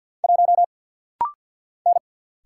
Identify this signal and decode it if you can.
Morse code, heard as a keyed tone, about 50 wpm, 700 Hz. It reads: 1 N
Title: Single Letter-Number - Rapid-Fire 50wpm